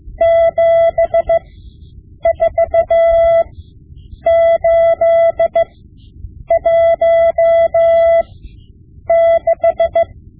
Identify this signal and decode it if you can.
Morse code, heard as a keyed tone, about 15 wpm, spaced as for 7 wpm, 670 Hz. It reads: 74816
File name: Morse Lessons / Morse Lessons / Tape-5B